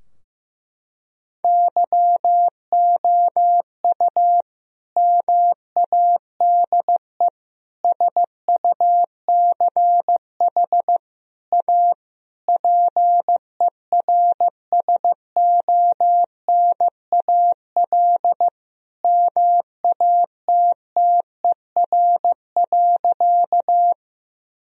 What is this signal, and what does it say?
Morse code, heard as a keyed tone, about 15 wpm, 700 Hz. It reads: YOU MADE SUCH A PERSONAL MATTER.
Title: Sentences from Top 1000 Words - Review Part 4 - 15wpm